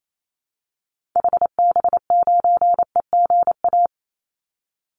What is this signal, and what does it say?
Morse code, heard as a keyed tone, about 28 wpm, 700 Hz. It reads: HB9EGA
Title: Call Signs - 28wpm